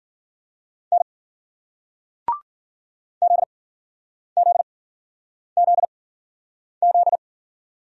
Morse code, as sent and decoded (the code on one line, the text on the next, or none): -. / --.. --.. --.. --..
N ZZZZ